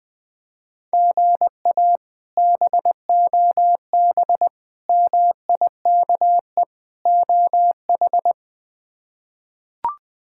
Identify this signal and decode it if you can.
Morse code, heard as a keyed tone, about 21 wpm, 700 Hz. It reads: GA BOB MIKE OH
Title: K1USN SST QSO Practice - 20wpm